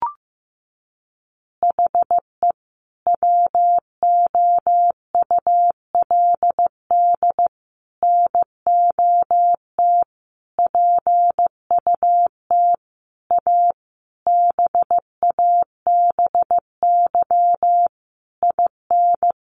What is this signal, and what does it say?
Morse code, heard as a keyed tone, about 15 wpm, 700 Hz. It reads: HE WOULD NOT PUT A BABY IN
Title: Easy Phrases 15wpm